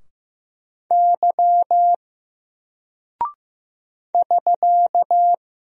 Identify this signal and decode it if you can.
Morse code, heard as a keyed tone, about 15 wpm, 700 Hz. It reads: Y <SK>
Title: Single Character (Letter:Number:Punctuation:Prosign) 15wpm